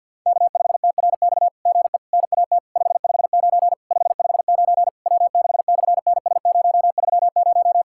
Morse code, heard as a keyed tone, about 50 wpm, 700 Hz. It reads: K5TRX GE NAT 559 559 W6/NS030
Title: SOTA QSO Practice 50wpm